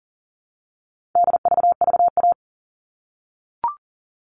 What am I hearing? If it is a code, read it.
Morse code, heard as a keyed tone, about 40 wpm, 700 Hz. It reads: DVVA
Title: All Cut Numbers - 4-Digit 40wpm